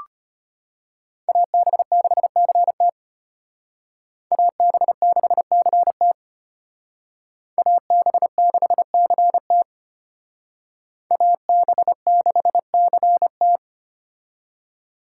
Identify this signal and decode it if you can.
Morse code, heard as a keyed tone, about 32 wpm, 700 Hz. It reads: AB6CT AB6CT AB6CT AB6CT